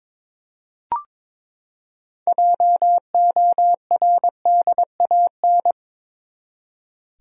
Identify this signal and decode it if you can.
Morse code, heard as a keyed tone, about 22 wpm, 700 Hz. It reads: JORDAN
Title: USA Common Names - 22wpm